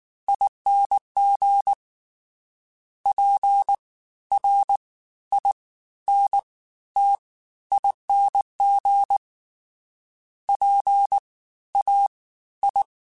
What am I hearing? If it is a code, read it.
Morse code, heard as a keyed tone, about 19 wpm, spaced as for 13 wpm, 800 Hz. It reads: ING PRINTING PAI
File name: Special INGs-mp3s / Special INGs-mp3s / Fast ing @18